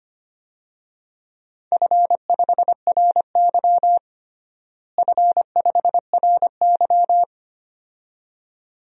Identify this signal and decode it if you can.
Morse code, heard as a keyed tone, about 25 wpm, 700 Hz. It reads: F5RY F5RY
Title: DXCC-100 Call Signs - 4 Characters - Repeated 2x - 25wpm